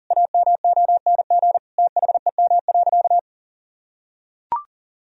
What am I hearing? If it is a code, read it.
Morse code, heard as a keyed tone, about 40 wpm, 700 Hz. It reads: AMONG THEM.
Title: Sentences from Top 1000 Words - Review Part 3 - 40wpm